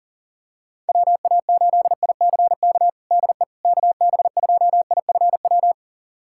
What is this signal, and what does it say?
Morse code, heard as a keyed tone, about 40 wpm, 700 Hz. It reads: WA8ICK DE KB2IFW
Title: USA Callsign DE Callsign 40wpm